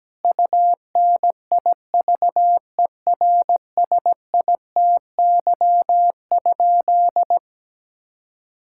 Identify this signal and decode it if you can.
Morse code, heard as a keyed tone, about 17 wpm, 700 Hz. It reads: UNIVERSITY?
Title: Sentences from Top 1000 Words - Review Part 5 - 17wpm